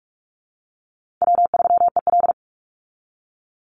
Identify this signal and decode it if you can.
Morse code, heard as a keyed tone, about 44 wpm, 700 Hz. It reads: W3EL
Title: Easy Call Signs Repeated 3x - 45wpm